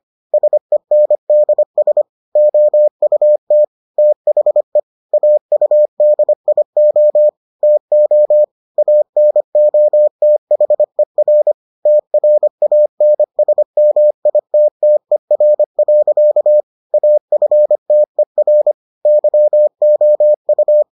Morse code, heard as a keyed tone, about 25 wpm, 600 Hz. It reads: SENDS OUT THE AUDIO TO ANOTHER TRANSMITTER. AFTER YOU